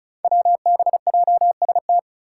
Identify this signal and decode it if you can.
Morse code, heard as a keyed tone, about 34 wpm, 700 Hz. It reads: WBJST
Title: Five Challenging Letter-Number Combinations 35wpm